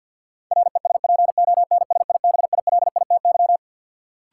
Morse code, heard as a keyed tone, about 52 wpm, 700 Hz. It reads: RESPONSIBILITY
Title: Sentences from Top 1200 Words - New Word Drill Alt - 50wpm